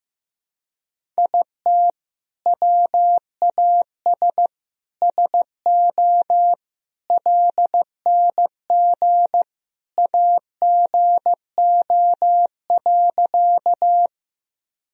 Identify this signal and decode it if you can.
Morse code, heard as a keyed tone, about 15 wpm, 700 Hz. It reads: IT WAS SO LNG AGO.
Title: Sentences from Top 500 Words and Abbreviations 15wpm